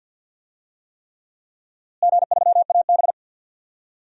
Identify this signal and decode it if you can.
Morse code, heard as a keyed tone, about 46 wpm, 700 Hz. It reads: G3AB
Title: DXCC-100 Call Signs - 4 Characters - Repeated 3x - 50wpm